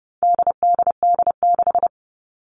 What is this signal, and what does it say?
Morse code, heard as a keyed tone, about 30 wpm, 700 Hz. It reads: DDD6
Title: All Cut Numbers - 4-Digit 30wpm